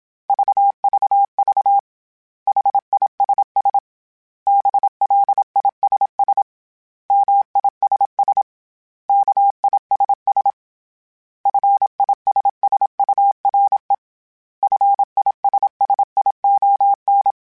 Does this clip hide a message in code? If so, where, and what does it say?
Yes: Morse code from 0.3 to 17.3 s, heard as a keyed tone, about 26 wpm, 800 Hz. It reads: VVV HISS BLISS MISS KISS FISSURE FISSION